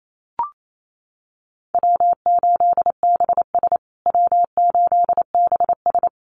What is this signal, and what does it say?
Morse code, heard as a keyed tone, about 28 wpm, 700 Hz. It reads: W8BS W8BS
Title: USA Call Signs Repeated 2x 28wpm